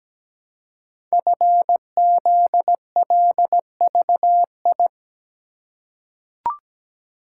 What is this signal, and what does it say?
Morse code, heard as a keyed tone, about 17 wpm, 700 Hz. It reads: FZLVI
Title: Code Groups - 5 Letters - 17wpm